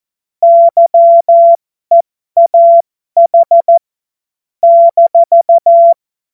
Morse code, heard as a keyed tone, about 14 wpm, spaced as for 12 wpm, 680 Hz. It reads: YEAH -
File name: Morsekurs / Witze